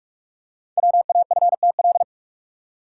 Morse code, heard as a keyed tone, about 44 wpm, 700 Hz. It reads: WAFTL
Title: Code Groups - 5 Letters - 45wpm